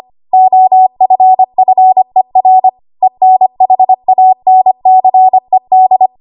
Morse code, heard as a keyed tone, about 25 wpm, 750 Hz. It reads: OFFER ENHANCED
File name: Archive / 25 WPM / 25WPM